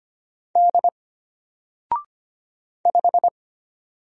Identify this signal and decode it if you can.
Morse code, heard as a keyed tone, about 25 wpm, 700 Hz. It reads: D 5
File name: Single-Letter-Number-25wpm